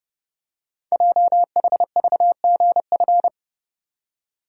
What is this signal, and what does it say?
Morse code, heard as a keyed tone, about 28 wpm, 690 Hz. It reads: JHVGF
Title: Code Groups - 5 Letters - 30wpm